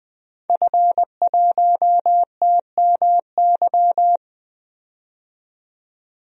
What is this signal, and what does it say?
Morse code, heard as a keyed tone, about 21 wpm, 700 Hz. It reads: F1TMY
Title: DXCC-100 Call Signs - 5 Characters - 20wpm